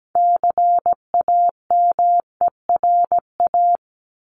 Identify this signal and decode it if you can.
Morse code, heard as a keyed tone, about 17 wpm, 700 Hz. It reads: CAMERA
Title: Top 501-750 Words 17wpm